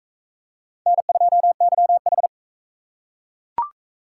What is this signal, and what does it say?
Morse code, heard as a keyed tone, about 37 wpm, 700 Hz. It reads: N2YH